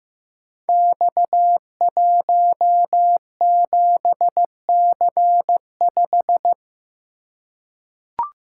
Morse code, heard as a keyed tone, about 15 wpm, 700 Hz. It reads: X17C5
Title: Code Groups - 5 Letter:Numbers - 15wpm